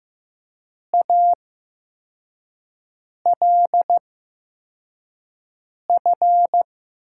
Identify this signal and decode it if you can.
Morse code, heard as a keyed tone, about 15 wpm, spaced as for 4 wpm, 700 Hz. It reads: ALF
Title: USA Call Signs - 6 Characters - 8x Character Spacing - 15wpm